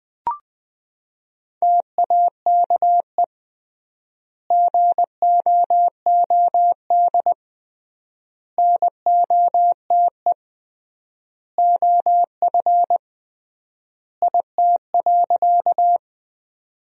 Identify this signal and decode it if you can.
Morse code, heard as a keyed tone, about 20 wpm, 700 Hz. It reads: TAKE GOOD NOTE OF IT.
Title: 3x Word-Spacing - Sentences from Top 500 Words - 20wpm